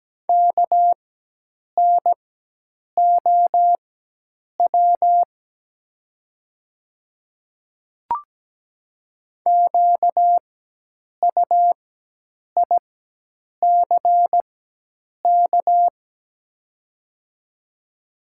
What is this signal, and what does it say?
Morse code, heard as a keyed tone, about 17 wpm, spaced as for 8 wpm, 700 Hz. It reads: KNOW QUICK
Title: 4x Character Spacing - Top 500 Words - 17wpm